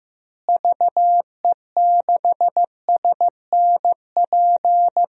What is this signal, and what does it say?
Morse code, heard as a keyed tone, about 15 wpm, 700 Hz. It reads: VE6SNP
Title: DXCC-100 Call Signs - 6 Characters - Repeated 2x - 15wpm